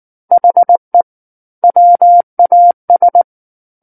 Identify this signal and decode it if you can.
Morse code, heard as a keyed tone, about 19 wpm, 700 Hz. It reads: HE WAS
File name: Book One / 10-Chapter 10 - 19 WPM